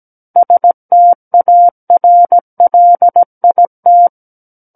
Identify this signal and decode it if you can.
Morse code, heard as a keyed tone, about 17 wpm, 700 Hz. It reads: STARLIT
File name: Book One / 08-Chapter 8 - 17 WPM